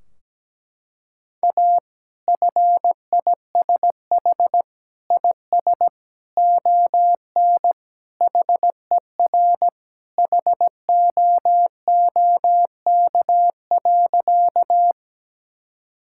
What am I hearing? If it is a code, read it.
Morse code, heard as a keyed tone, about 17 wpm, 700 Hz. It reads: A FISH IS ON HER HOOK.
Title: Sentences from Top 1600 Words 17wpm